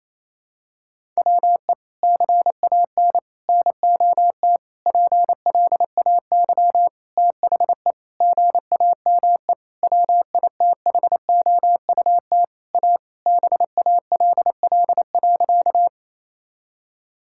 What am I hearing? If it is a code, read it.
Morse code, heard as a keyed tone, about 28 wpm, 700 Hz. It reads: WE CAN NOT PLAY THE GAME WITHOUT A BALL.
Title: Sentences from Top 1000 Words - Review Part 6 - 28wpm